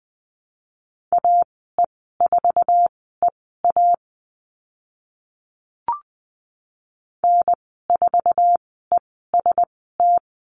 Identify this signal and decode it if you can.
Morse code, heard as a keyed tone, about 20 wpm, spaced as for 15 wpm, 700 Hz. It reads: AE4EA N4EST
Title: 069 Lesson 12g - Letter d - Call Signs 20wpm